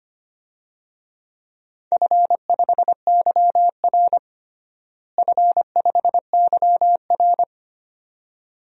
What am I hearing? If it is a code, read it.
Morse code, heard as a keyed tone, about 25 wpm, 700 Hz. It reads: F5YR F5YR
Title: DXCC-100 Call Signs - 4 Characters - Repeated 2x - 25wpm